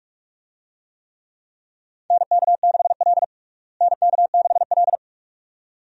Morse code, heard as a keyed tone, about 42 wpm, 700 Hz. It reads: NK6L NK6L
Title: USA Call Signs - 4 Characters - Repeated 2x - 45wpm